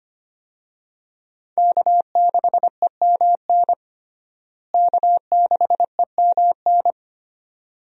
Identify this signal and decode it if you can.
Morse code, heard as a keyed tone, about 25 wpm, 700 Hz. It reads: K6EMN K6EMN
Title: USA Call Signs - 5 Characters - Repeated 2x - 25wpm